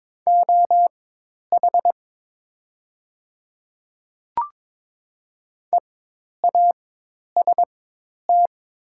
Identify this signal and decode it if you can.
Morse code, heard as a keyed tone, about 22 wpm, spaced as for 10 wpm, 700 Hz. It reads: OH EAST
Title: 4x Character Spacing - Top 500 Words - 22wpm